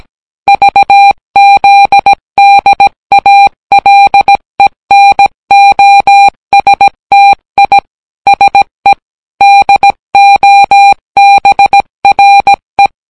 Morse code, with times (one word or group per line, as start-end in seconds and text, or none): 0.5-7.8 VZDALENOSTI
8.3-8.9 SE
9.4-12.9 DOBRE